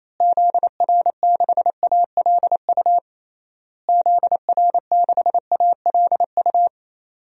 Morse code, heard as a keyed tone, about 28 wpm, 700 Hz. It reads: ZR6ALU ZR6ALU
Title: DXCC-100 Call Signs - 6 Characters - Repeated 3x - 28wpm